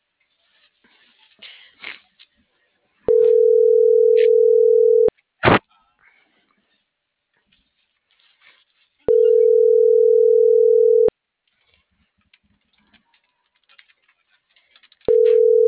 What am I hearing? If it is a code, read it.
Ringback tone (North America), 440 + 480 Hz.